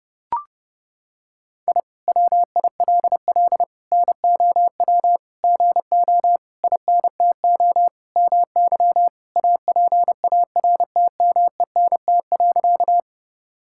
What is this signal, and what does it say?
Morse code, heard as a keyed tone, about 30 wpm, 700 Hz. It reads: I WILL NOW GO INTO MY APARTMENT.
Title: Sentences from Top 1000 Words - Review Part 6 - 30wpm